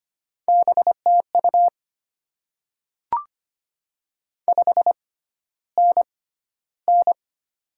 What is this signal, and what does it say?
Morse code, heard as a keyed tone, about 25 wpm, spaced as for 13 wpm, 690 Hz. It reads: BTU 5NN
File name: Top 20 QSO Elements - 6x Character Spacing - 25wpm